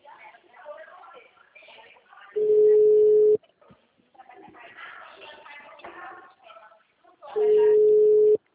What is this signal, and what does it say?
Ringback tone (Europe, 425 Hz), with speech.